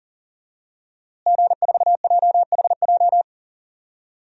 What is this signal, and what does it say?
Morse code, heard as a keyed tone, about 40 wpm, 700 Hz. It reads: G4JHJ
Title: DXCC-100 Call Signs - 5 Characters - Repeated 3x - 40wpm